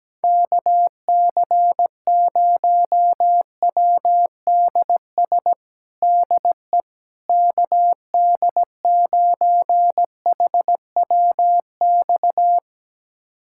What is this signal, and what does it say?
Morse code, heard as a keyed tone, about 17 wpm, 700 Hz. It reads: KC0WDS DE KD9HWX
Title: USA Callsign DE Callsign - 6 Characters - 17wpm